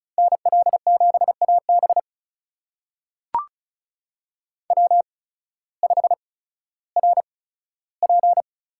Morse code, heard as a keyed tone, about 36 wpm, spaced as for 16 wpm, 690 Hz. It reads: NL7AB W5RP
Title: USA Call Signs - 5 Characters - 8x Character Spacing - 35wpm